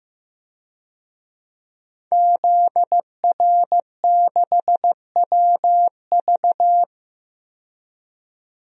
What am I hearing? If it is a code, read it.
Morse code, heard as a keyed tone, about 15 wpm, 700 Hz. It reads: ZR6WV
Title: DXCC-100 Call Signs - 5 Characters - 15wpm